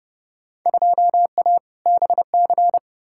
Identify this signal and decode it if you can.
Morse code, heard as a keyed tone, about 30 wpm, 700 Hz. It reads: 2A BC
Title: Field Day Exchanges Encore - 30wpm